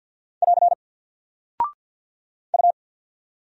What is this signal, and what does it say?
Morse code, heard as a keyed tone, about 48 wpm, 700 Hz. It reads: <AR> U
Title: Single Character (Letter:Number:Punctuation:Prosign) 50wpm